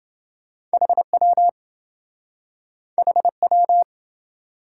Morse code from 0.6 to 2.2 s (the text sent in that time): HW